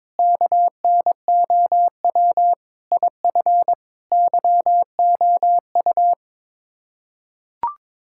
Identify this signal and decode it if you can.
Morse code, heard as a keyed tone, about 22 wpm, 700 Hz. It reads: KNOW IF YOU
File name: practice / 22wpm / Build A Phrase - 5-Word Phrases - 22wpm